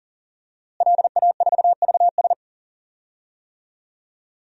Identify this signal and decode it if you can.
Morse code, heard as a keyed tone, about 37 wpm, 700 Hz. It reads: LA4VS
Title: DXCC-100 Call Signs - 5 Characters - Repeated 2x - 40wpm